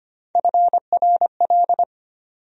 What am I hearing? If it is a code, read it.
Morse code, heard as a keyed tone, about 25 wpm, 700 Hz. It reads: FRL
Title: USA Call Sign Postfixes - 3 Letters 25wpm